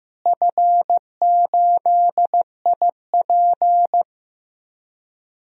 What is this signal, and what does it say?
Morse code, heard as a keyed tone, about 15 wpm, 700 Hz. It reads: F8IP
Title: DXCC-100 Call Signs - 4 Characters - Repeated 2x - 15wpm